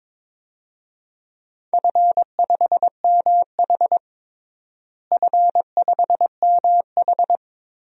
Morse code, heard as a keyed tone, about 22 wpm, 700 Hz. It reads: F5MH F5MH
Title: DXCC-100 Call Signs - 4 Characters - Repeated 3x - 22wpm